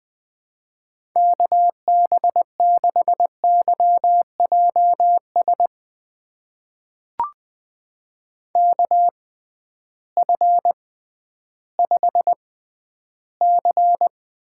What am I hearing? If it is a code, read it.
Morse code, heard as a keyed tone, about 21 wpm, spaced as for 12 wpm, 700 Hz. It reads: KB6YJS KF5C
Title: USA Call Signs - 6x Character Spacing - 20wpm